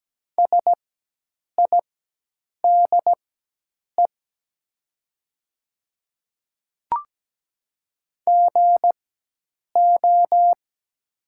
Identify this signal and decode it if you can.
Morse code, heard as a keyed tone, about 17 wpm, spaced as for 8 wpm, 700 Hz. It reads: SIDE GO
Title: 4x Character Spacing - Top 100 Words - 17wpm